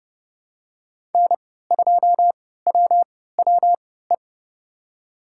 Morse code, heard as a keyed tone, about 32 wpm, spaced as for 17 wpm, 700 Hz. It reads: N2WWE
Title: USA Call Signs - 5 Characters - 3x Character Spacing - 30wpm